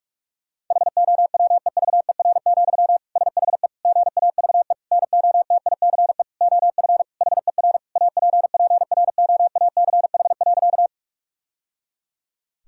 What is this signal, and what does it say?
Morse code, heard as a keyed tone, about 46 wpm, 700 Hz. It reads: HOWEVER, SHE GAVE NOTICE OF HER APPROACH.
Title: Sentences from Top 1000 Words - Review Part 6 - 45wpm